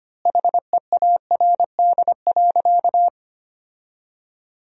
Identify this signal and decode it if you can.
Morse code, heard as a keyed tone, about 25 wpm, 700 Hz. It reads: HEARD.
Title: 4x Word-Spacing - Sentences from Top 400 Words - 25wpm